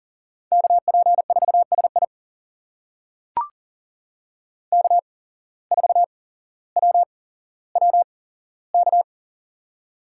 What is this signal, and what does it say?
Morse code, heard as a keyed tone, about 40 wpm, spaced as for 17 wpm, 700 Hz. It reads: KP4SI K4WWK
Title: USA Call Signs - 8x Character Spacing - 40wpm